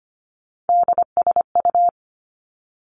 Morse code, heard as a keyed tone, about 25 wpm, 700 Hz. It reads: DSU
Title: Three Challenging Letter-Number Combinations 25wpm